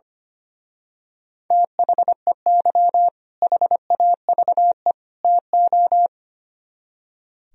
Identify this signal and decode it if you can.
Morse code, heard as a keyed tone, about 28 wpm, 700 Hz. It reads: THEY HAVE TO
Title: Speed-Racing - Sets of 3 Words 25wpm